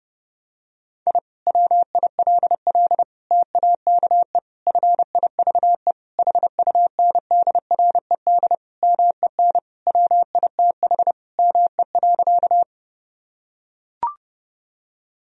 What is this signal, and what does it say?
Morse code, heard as a keyed tone, about 32 wpm, 700 Hz. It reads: I WILL TAKE FIVE HUNDRED MEN WITH ME.
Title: Sentences from Top 1000 Words - Review Part 2 - 30wpm